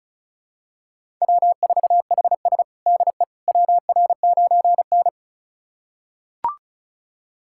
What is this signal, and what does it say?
Morse code, heard as a keyed tone, about 34 wpm, 700 Hz. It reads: W4HS DE WR9N
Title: USA Callsign DE Callsign - 4 Characters - 35wpm